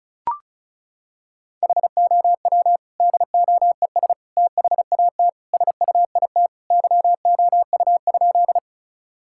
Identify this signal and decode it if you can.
Morse code, heard as a keyed tone, about 36 wpm, 700 Hz. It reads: HOW DOES THAT SUIT YOU?
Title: Sentences from Top 1600 Words 35wpm